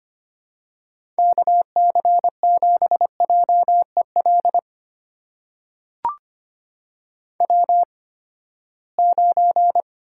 Morse code, heard as a keyed tone, about 25 wpm, spaced as for 17 wpm, 700 Hz. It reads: KC7JEL W9
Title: USA Call Signs - 8x Character Spacing - 25wpm